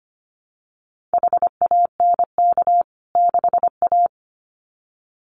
Morse code, heard as a keyed tone, about 25 wpm, 700 Hz. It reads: HANK 6A
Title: CWT Exchange 25wpm